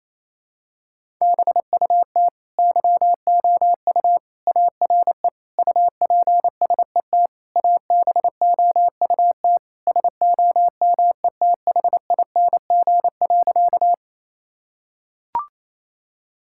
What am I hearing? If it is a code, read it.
Morse code, heard as a keyed tone, about 27 wpm, 700 Hz. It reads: BUT YOU ARE UPSET ABOUT SOMETHING.
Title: Sentences from Top 1000 Words - Review Part 5 - 28wpm